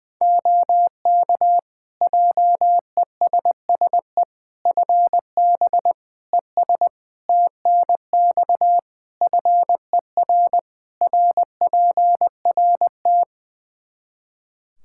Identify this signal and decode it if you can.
Morse code, heard as a keyed tone, about 20 wpm, 700 Hz. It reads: OK JESSE FB ES TNX FER RPRT
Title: Practice QSO - Exchange 2 - Info - 20wpm